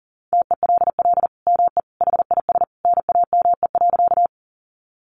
Morse code, heard as a keyed tone, about 40 wpm, 700 Hz. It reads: TELL ME HIS NAME.
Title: Sentences from Top 200 Words 40wpm